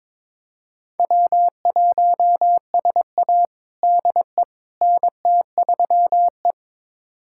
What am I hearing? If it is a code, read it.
Morse code, heard as a keyed tone, about 22 wpm, 700 Hz. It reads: W1SA DE NT3E
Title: USA Callsign DE Callsign - 4 Characters - 22wpm